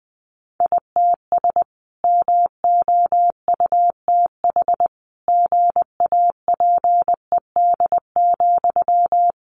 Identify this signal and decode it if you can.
Morse code, heard as a keyed tone, about 19 wpm, 700 Hz. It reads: ITS MOUTH GAPED,
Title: A Sound Of Thunder 20wpm